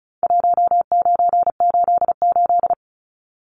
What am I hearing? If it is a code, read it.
Morse code, heard as a keyed tone, about 34 wpm, 700 Hz. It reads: 1988
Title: Numbers - Contest Style - 4-Digit 35wpm